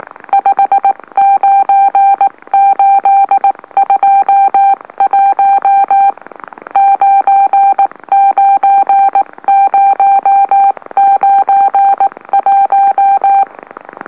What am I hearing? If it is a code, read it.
Morse code, heard as a keyed tone, about 18 wpm, 790 Hz. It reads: 59821 99091
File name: M04- -2345utc-LOLO